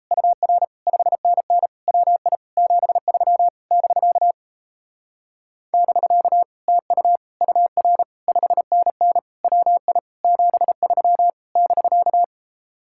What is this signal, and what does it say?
Morse code, heard as a keyed tone, about 38 wpm, 700 Hz. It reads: UR 5NN WI 73 <BK> <BK> TU UR 5NN WI 73 <BK>